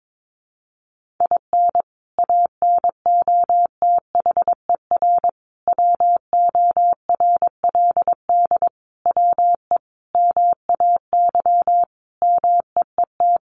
Morse code, heard as a keyed tone, about 22 wpm, 700 Hz. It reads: IN ANOTHER WORLD WE MAY MEET
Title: Sentences from Top 1100 Words 22wpm